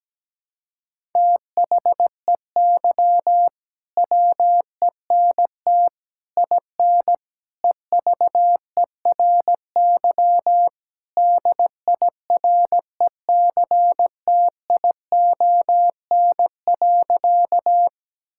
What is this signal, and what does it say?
Morse code, heard as a keyed tone, about 17 wpm, 700 Hz. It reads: THEY WENT IN EVERY DIRECTION.
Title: Sentences from Top 1000 Words - Review Part 5 - 17wpm